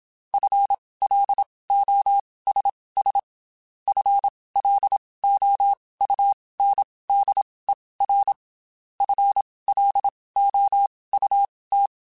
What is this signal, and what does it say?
Morse code, heard as a keyed tone, about 27 wpm, spaced as for 19 wpm, 800 Hz. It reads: FLOSS FLOUNDER FLOUT
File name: Spaced / FL Runs sp mp3s / FL Runs sp @25